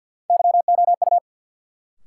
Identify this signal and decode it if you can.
Morse code, heard as a keyed tone, about 53 wpm, 700 Hz. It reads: YOU
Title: Sets of 2 Words - New Word Drill - 50wpm